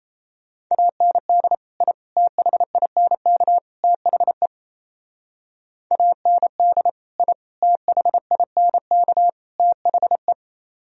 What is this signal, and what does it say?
Morse code, heard as a keyed tone, about 36 wpm, 700 Hz. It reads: AND I THINK THE AND I THINK THE